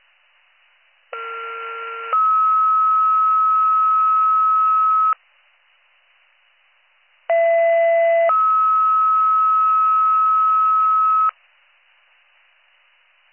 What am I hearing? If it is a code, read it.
Two-tone page, heard as a one-second tone, then a three-second tone, about 1370 then 1290 Hz, about 670 then 1290 Hz. It is heard from 1.1 to 11.3 s.